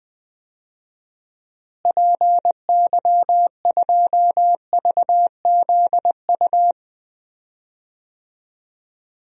Morse code, heard as a keyed tone, about 20 wpm, 700 Hz. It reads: PY2VZU
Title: DXCC-100 Call Signs - 6 Characters - Repeated 3x - 20wpm